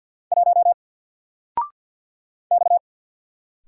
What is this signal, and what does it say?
Morse code, heard as a keyed tone, about 47 wpm, 700 Hz. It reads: 1 X
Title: Single Letter-Number 50wpm